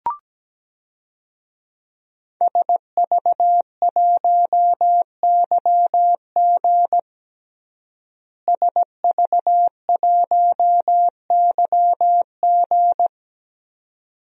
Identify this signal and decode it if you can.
Morse code, heard as a keyed tone, about 17 wpm, 700 Hz. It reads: SV1YG SV1YG
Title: DXCC-100 Call Signs - 5 Characters - Repeated 2x - 17wpm